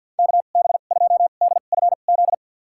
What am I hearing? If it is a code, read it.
Morse code, heard as a keyed tone, about 45 wpm, spaced as for 34 wpm, 700 Hz. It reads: KB2DFZ
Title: USA Call Signs - 2x Character Spacing - 50wpm